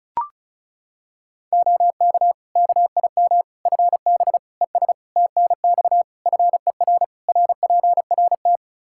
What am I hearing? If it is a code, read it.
Morse code, heard as a keyed tone, about 34 wpm, 700 Hz. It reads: OK KIM FB ES TNX FER RPRT
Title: Practice QSO - Exchange 2 - Info - 35wpm